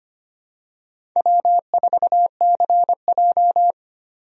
Morse code, heard as a keyed tone, about 25 wpm, 700 Hz. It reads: W4CJ
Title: USA Call Signs - 4 Characters - 25wpm